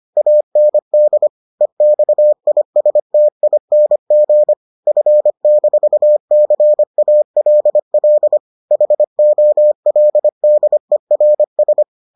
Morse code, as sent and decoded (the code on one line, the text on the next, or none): .- -. -.. / . -..- .. ... - .. -. --. / ..-. -....- -.-. .- .-.. .-.. / .... --- .-.. -.. .-. ...
AND EXISTING F-CALL HOLDERS